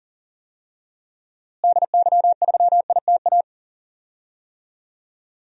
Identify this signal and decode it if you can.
Morse code, heard as a keyed tone, about 38 wpm, 700 Hz. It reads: DY3ITA
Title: DXCC-100 Call Signs - 6 Characters - Repeated 3x - 40wpm